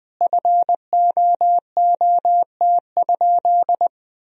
Morse code, heard as a keyed tone, about 20 wpm, 700 Hz. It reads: FOOT?
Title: Sentences from Top 500 Words and Abbreviations 20wpm